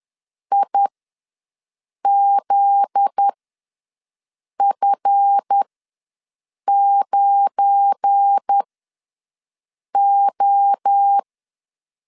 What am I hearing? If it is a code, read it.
Morse code, heard as a keyed tone, about 10 wpm, spaced as for 5 wpm, 800 Hz. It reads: IZF9O
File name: LEZIONI / 7 GF9Z / 7 GF9Z ws 10wpm 3919